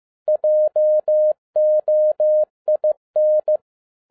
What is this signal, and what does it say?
Morse code, heard as a keyed tone, about 15 wpm, 610 Hz. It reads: JOIN
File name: 2011-07-mp3 / n47 JOHN K3TN 15wpm